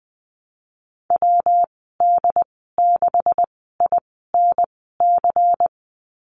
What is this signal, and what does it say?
Morse code, heard as a keyed tone, about 20 wpm, spaced as for 14 wpm, 700 Hz. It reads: WD6INC
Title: 238 Lesson 38g - Letter q - Call Signs 20wpm